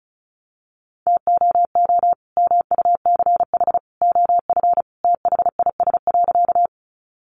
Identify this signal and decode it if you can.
Morse code, heard as a keyed tone, about 34 wpm, 700 Hz. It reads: TOO MUCH OF THIS.
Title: Sentences from Top 200 Words 35wpm